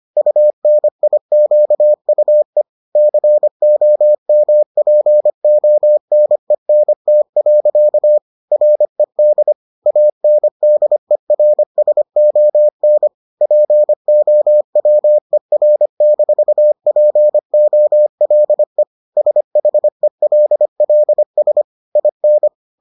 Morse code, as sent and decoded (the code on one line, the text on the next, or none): ..- -. .. --.- ..- . / -.-. --- -- .--. --- -. -. - .-.-.- / .-. -.. / .- -. -.. .-. ... --- -. / .--. --- .-- . .-. -....- .--. --- .-.. / ... .... .-.. .-.. ... / .. -.
UNIQUE COMPONENT. RED ANDERSON POWER-POLE SHELLS IN